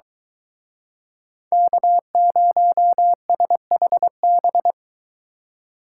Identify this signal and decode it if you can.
Morse code, heard as a keyed tone, about 23 wpm, 700 Hz. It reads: K0SHB